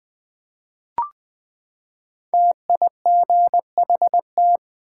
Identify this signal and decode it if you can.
Morse code, heard as a keyed tone, about 21 wpm, 700 Hz. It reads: TIGHT
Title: Five Letter Words - 20wpm